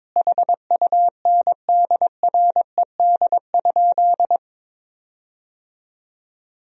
Morse code, heard as a keyed tone, about 23 wpm, 700 Hz. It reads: HUNDRED?
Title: 4x Word-Spacing - Sentences from Top 400 Words - 22wpm